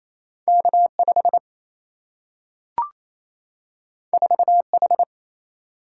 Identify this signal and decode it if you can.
Morse code, heard as a keyed tone, about 28 wpm, 700 Hz. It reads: K5 4H
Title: Single-Letter Single-Number 28wpm